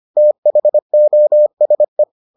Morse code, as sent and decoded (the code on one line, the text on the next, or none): - .... --- ... .
THOSE